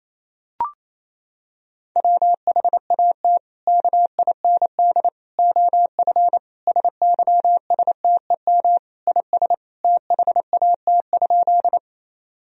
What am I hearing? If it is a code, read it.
Morse code, heard as a keyed tone, about 28 wpm, 700 Hz. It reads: WHAT KIND OF SYSTEM IS THAT?
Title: Sentences from Top 1000 Words - Review Part 3 - 28wpm